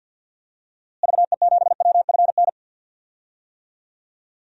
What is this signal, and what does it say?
Morse code, heard as a keyed tone, about 50 wpm, 700 Hz. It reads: VE7WFN
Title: DXCC-100 Call Signs - 6 Characters - Repeated 2x - 50wpm